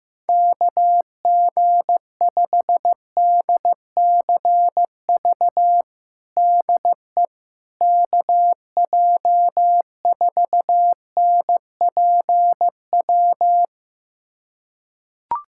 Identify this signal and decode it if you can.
Morse code, heard as a keyed tone, about 16 wpm, 700 Hz. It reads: KG5DCV DE KJ4NPW
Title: USA Callsign DE Callsign - 6 Characters - 15wpm